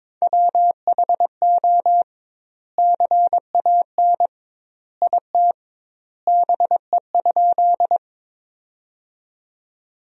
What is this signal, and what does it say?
Morse code, heard as a keyed tone, about 22 wpm, 700 Hz. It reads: WHO CAN IT BE?
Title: 2x Word-Spacing - Sentences from Top 100 Words - 22wpm